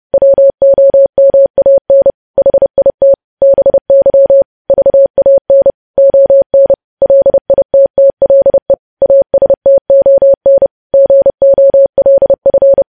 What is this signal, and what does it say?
Morse code, heard as a keyed tone, about 30 wpm, 550 Hz. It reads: WOMAN HIT BY VAN ON LITTLE ASTON GOLF
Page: news headlines in morse code at 30 wpm